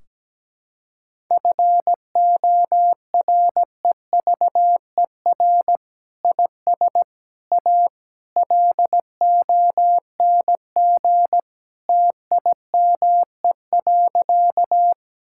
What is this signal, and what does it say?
Morse code, heard as a keyed tone, about 17 wpm, 700 Hz. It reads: FOREVER IS A LONG TIME.
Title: Sentences from Top 1600 Words 17wpm